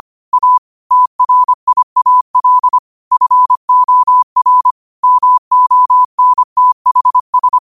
Morse code, heard as a keyed tone, about 25 wpm, 1000 Hz. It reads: A TRIAL FOR MONTHS
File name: qotdfiles / 25wpm